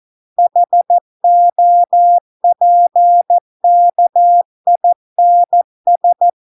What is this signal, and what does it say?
Morse code, heard as a keyed tone, about 14 wpm, 700 Hz. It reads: HOPKINS